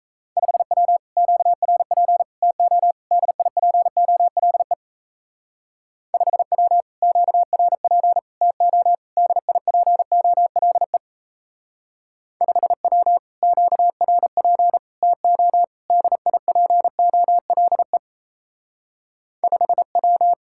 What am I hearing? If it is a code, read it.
Morse code, heard as a keyed tone, about 36 wpm, 700 Hz. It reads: 5W QRP TO DIPOLE 5W QRP TO DIPOLE 5W QRP TO DIPOLE 5W